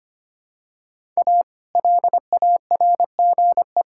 Morse code, heard as a keyed tone, about 25 wpm, 700 Hz. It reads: A LARGE
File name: practice / 25wpm / Sentences with 4-Word Phrases - 25wpm